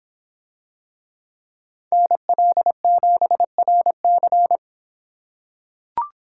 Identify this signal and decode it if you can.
Morse code, heard as a keyed tone, about 24 wpm, 700 Hz. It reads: NL7RC